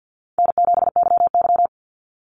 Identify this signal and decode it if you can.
Morse code, heard as a keyed tone, about 50 wpm, 700 Hz. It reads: N7YY
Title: NA SOTA Chaser Callsigns 50wpm